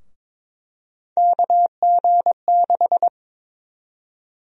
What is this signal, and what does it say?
Morse code, heard as a keyed tone, about 21 wpm, 700 Hz. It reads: KG6
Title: USA Call Sign Prefixes - Repeated 2x - 22wpm